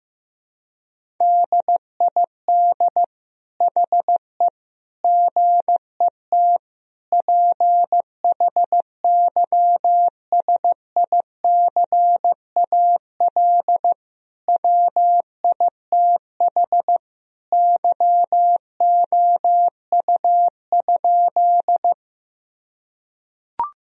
Morse code, heard as a keyed tone, about 15 wpm, 700 Hz. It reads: DID HE GET PHYSICAL WITH YOU?
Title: Sentences from Top 1000 Words - Review Part 4 - 15wpm